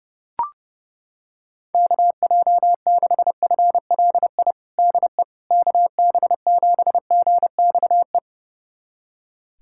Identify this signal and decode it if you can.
Morse code, heard as a keyed tone, about 30 wpm, 700 Hz. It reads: KJ6FLI DE KB7GXE
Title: USA Callsign DE Callsign - 6 Characters - 30wpm